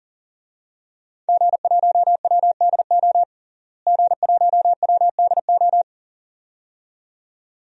Morse code, heard as a keyed tone, about 37 wpm, 700 Hz. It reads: G1WDO G1WDO
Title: DXCC-100 Call Signs - 5 Characters - Repeated 2x - 40wpm